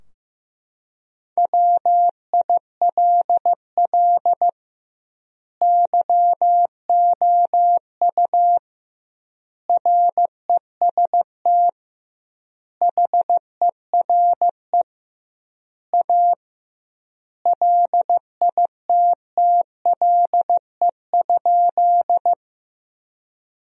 Morse code, heard as a keyed tone, about 16 wpm, 700 Hz. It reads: WILL YOU REST HERE A LITTLE?
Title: 2x Word-Spacing - Sentences from Top 500 Words - 15wpm